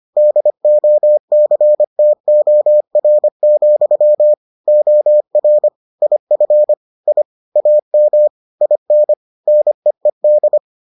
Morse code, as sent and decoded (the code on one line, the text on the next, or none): -.. --- -.-. - --- .-. --..-- / --- .-. / .. ..-. / .. / .- -- / .. -. / -. . -..
DOCTOR, OR IF I AM IN NEED